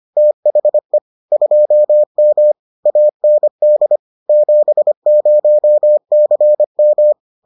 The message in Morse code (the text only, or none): THE 2M AND 70CM